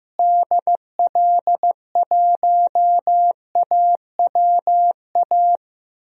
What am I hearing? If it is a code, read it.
Morse code, heard as a keyed tone, about 15 wpm, 700 Hz. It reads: DL1AWA
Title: DXCC-100 Call Signs - 6 Characters - Repeated 3x - 15wpm